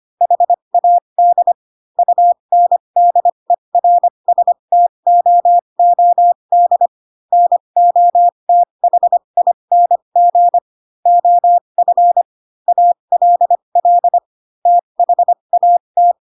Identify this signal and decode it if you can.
Morse code, heard as a keyed tone, about 25 wpm, spaced as for 22 wpm, 700 Hz. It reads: HAD UNDERSTOOD NOTHING OF ALL THAT
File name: Pinocchio / Pinocchio ch7 18 25 ik1hge